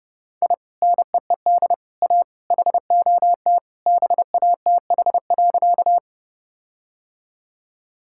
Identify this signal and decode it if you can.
Morse code, heard as a keyed tone, about 32 wpm, 700 Hz. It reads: I NEED A HOT BATH.
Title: Sentences from Top 1600 Words 30wpm